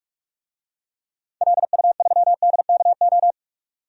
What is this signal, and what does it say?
Morse code, heard as a keyed tone, about 46 wpm, 700 Hz. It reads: LU3DKO
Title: DXCC-100 Call Signs - 6 Characters - 45wpm